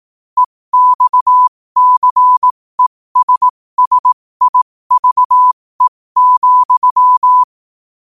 Morse code, heard as a keyed tone, about 18 wpm, spaced as for 15 wpm, 1000 Hz. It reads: EXCESSIVE,
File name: qotdfiles / 15wpm